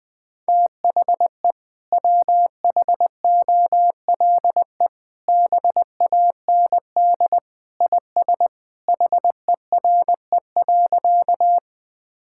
Morse code, heard as a keyed tone, about 21 wpm, 700 Hz. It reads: THE WHOLE BAND IS HERE.
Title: Sentences from Top 1600 Words 20wpm